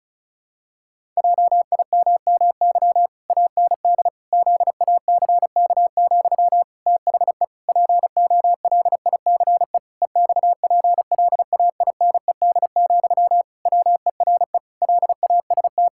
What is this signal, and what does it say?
Morse code, heard as a keyed tone, about 36 wpm, 700 Hz. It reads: JIMMY AND ZACK, THE POLICE EXPLAINED, WERE LAST
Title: Pangram Challenge v2 - 35wpm